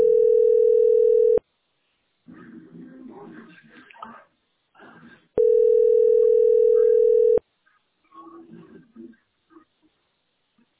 Ringback tone (North America), 440 + 480 Hz.